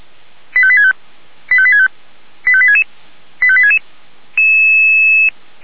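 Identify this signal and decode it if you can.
Five-tone selective call (ZVEI), heard as quick runs of five tones, tones of about 70 ms. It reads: 85865 85865 85866 85866
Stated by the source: ZVEI five-tone selective call 85863 received on SDR.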